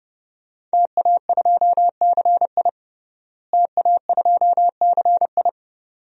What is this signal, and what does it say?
Morse code, heard as a keyed tone, about 30 wpm, 700 Hz. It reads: TA2CI TA2CI
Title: DXCC-100 Call Signs - 5 Characters - Repeated 2x - 30wpm